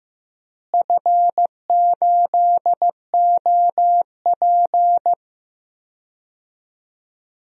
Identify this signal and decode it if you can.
Morse code, heard as a keyed tone, about 14 wpm, 700 Hz. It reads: F8OP
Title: DXCC-100 Call Signs - 4 Characters - Repeated 3x - 15wpm